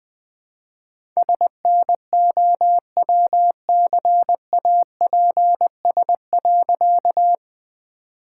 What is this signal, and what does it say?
Morse code, heard as a keyed tone, about 20 wpm, 700 Hz. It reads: SNOWCAPS.
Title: All-Age Appropriate Jokes - 20wpm